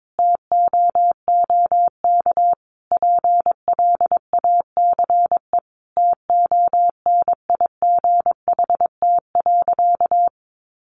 Morse code, heard as a keyed tone, about 22 wpm, 700 Hz. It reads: TOOK PLACE TONIGHT.
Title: Sentences from Top 1300 Words 22wpm